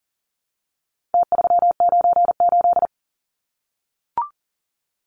Morse code, heard as a keyed tone, about 39 wpm, 700 Hz. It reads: T398
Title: Numbers - Contest Style - 4-Digit 40wpm